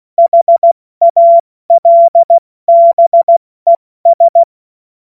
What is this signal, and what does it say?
Morse code, heard as a keyed tone, about 16 wpm, 680 Hz. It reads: HALBES
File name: Morsekurs / Witze / Witz 27. Januar 2026 70 F80